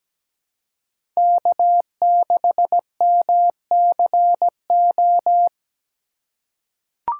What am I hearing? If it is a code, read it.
Morse code, heard as a keyed tone, about 17 wpm, 700 Hz. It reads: K6MCO
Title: USA Call Signs - 5 Characters - 3x Character Spacing - 17wpm